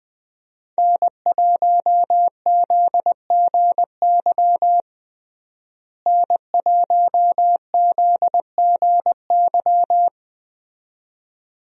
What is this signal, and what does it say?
Morse code, heard as a keyed tone, about 20 wpm, 700 Hz. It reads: N1ZGY N1ZGY
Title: USA Call Signs - 5 Characters - Repeated 2x - 20wpm